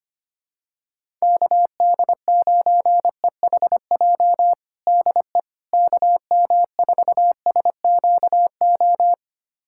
Morse code, heard as a keyed tone, about 25 wpm, 700 Hz. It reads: KD9EHJ DE KM4SQO
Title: USA Callsign DE Callsign 25wpm